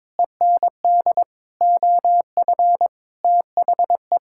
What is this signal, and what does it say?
Morse code, heard as a keyed tone, about 22 wpm, 700 Hz. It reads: END OF THE
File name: practice / 22wpm / Sets of 7 Words - 22wpm